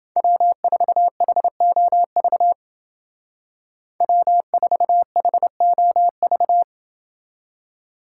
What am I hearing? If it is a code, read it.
Morse code, heard as a keyed tone, about 29 wpm, 700 Hz. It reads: W4HOV W4HOV